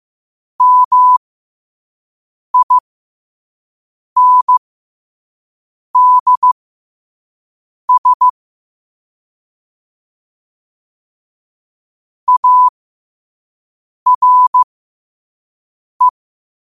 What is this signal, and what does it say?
Morse code, heard as a keyed tone, about 15 wpm, spaced as for 5 wpm, 1000 Hz. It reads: MINDS ARE
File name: qotdfiles / 5wpm